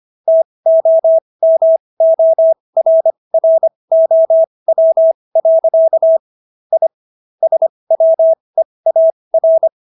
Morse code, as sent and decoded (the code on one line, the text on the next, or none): - --- -- --- .-. .-. --- .-- .-.-.- / .. / ... .-- . .- .-.
TOMORROW. I SWEAR